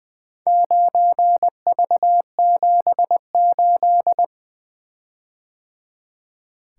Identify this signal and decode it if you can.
Morse code, heard as a keyed tone, about 21 wpm, 700 Hz. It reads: 9V78
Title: Four Letter-Number Combinations - 20wpm